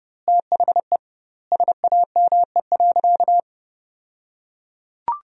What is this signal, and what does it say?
Morse code, heard as a keyed tone, about 30 wpm, 700 Hz. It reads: THE SAME.
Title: 2x Word-Spacing - Sentences from Top 300 Words - 30wpm